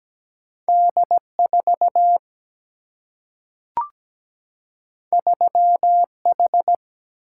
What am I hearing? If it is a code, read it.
Morse code, heard as a keyed tone, about 17 wpm, 700 Hz. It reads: D4 3H
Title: Single-Letter Single-Number 17wpm